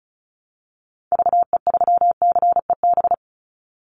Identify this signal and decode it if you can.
Morse code, heard as a keyed tone, about 34 wpm, 700 Hz. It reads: VE3CEB
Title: Call Signs Repeated 2x - 35wpm